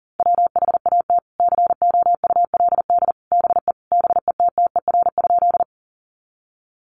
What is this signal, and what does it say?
Morse code, heard as a keyed tone, about 40 wpm, 700 Hz. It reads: WHAT COULD BE BETTER?
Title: Sentences from Top 400 Words 40wpm